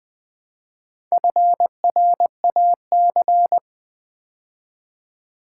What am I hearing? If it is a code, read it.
Morse code, heard as a keyed tone, about 20 wpm, 700 Hz. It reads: FRAC
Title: Build A Word - 2 Letters At A Time - 8-Letter Words - 20wpm